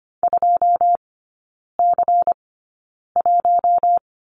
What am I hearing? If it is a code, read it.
Morse code, heard as a keyed tone, about 25 wpm, spaced as for 9 wpm, 700 Hz. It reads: 2C1
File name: course / 25:10wpm / 093 Lesson 16e - Letter c - Three Characters 25wpm